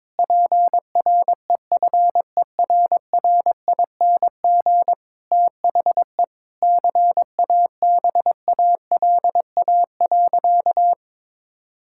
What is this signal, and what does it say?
Morse code, heard as a keyed tone, about 21 wpm, 700 Hz. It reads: PREFERRING THE CABALA.
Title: Pangram Challenge v2 - 22wpm